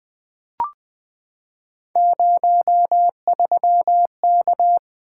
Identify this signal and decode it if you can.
Morse code, heard as a keyed tone, about 20 wpm, 700 Hz. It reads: 03K
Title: Three Letter-Number Combinations - 20wpm